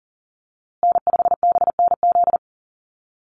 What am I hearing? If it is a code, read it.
Morse code, heard as a keyed tone, about 40 wpm, 700 Hz. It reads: N5BNZ
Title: USA Call Signs Repeated 3x 40wpm